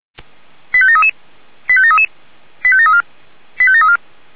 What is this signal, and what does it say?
Five-tone selective call (ZVEI), heard as quick runs of five tones, tones of about 70 ms. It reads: 85733 85733 85734 85734